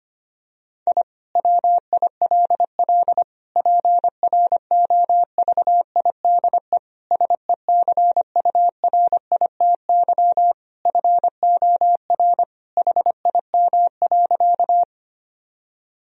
Morse code, heard as a keyed tone, about 25 wpm, 700 Hz. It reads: I WILL PROVIDE SECURITY FOR HIM.
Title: Sentences from Top 1000 Words - Review Part 4 - 25wpm